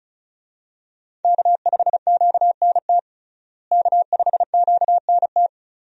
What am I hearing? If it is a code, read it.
Morse code, heard as a keyed tone, about 34 wpm, 700 Hz. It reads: K5QNT K5QNT
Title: USA Call Signs - 5 Characters - Repeated 2x - 35wpm